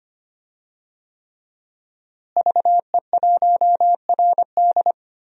Morse code, heard as a keyed tone, about 25 wpm, 700 Hz. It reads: VE1RD
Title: DXCC-100 Call Signs - 5 Characters - Repeated 3x - 25wpm